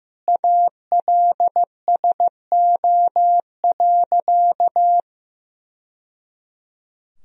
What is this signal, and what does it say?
Morse code, heard as a keyed tone, about 15 wpm, 700 Hz. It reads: ALSO.
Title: Sentences from Top 1000 Words - Review Part 3 - 15wpm